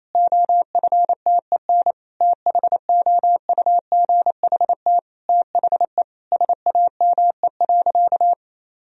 Morse code, heard as a keyed tone, about 28 wpm, 700 Hz. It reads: OFTEN THOUGHT THE SAME.
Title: Sentences from Top 1000 Words - Review Part 1 - 28wpm